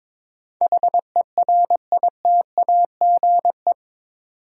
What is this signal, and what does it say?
Morse code, heard as a keyed tone, about 22 wpm, 700 Hz. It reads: HERITAGE
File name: practice / 22wpm / Eight Letter Words - 22wpm